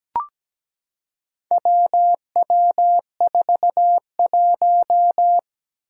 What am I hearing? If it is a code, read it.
Morse code, heard as a keyed tone, about 17 wpm, 700 Hz. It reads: WW41
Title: Four Challenging Letter-Number Combinations 17wpm